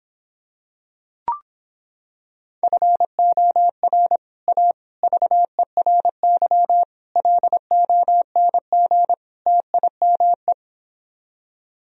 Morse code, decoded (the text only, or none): FOR A VERY LONG TIME